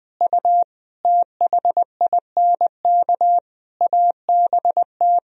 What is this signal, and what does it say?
Morse code, heard as a keyed tone, about 20 wpm, 700 Hz. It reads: U THINK ABT
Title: Sentences from Top 200 Words and Abbreviations 20wpm